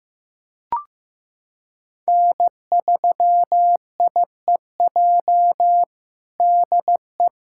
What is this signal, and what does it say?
Morse code, heard as a keyed tone, about 15 wpm, 700 Hz. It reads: N3IEJ DE
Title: USA Callsign DE Callsign 15wpm